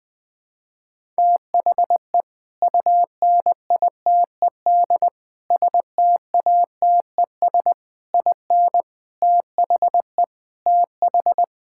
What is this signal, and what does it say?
Morse code, heard as a keyed tone, about 20 wpm, 700 Hz. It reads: THE UNITED STATES IN THE TH
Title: Sentences with 5-Word Phrases - 20wpm